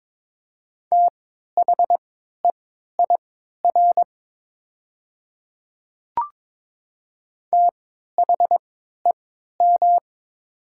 Morse code, heard as a keyed tone, about 22 wpm, spaced as for 12 wpm, 700 Hz. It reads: THEIR THEM
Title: 3x Character Spacing - Top 100 Words - 22wpm